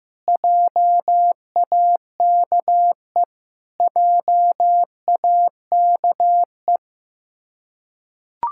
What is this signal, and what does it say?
Morse code, heard as a keyed tone, about 15 wpm, 700 Hz. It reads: JAKE JAKE
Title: Practice QSO - Exchange 1 - Beginning - 15wpm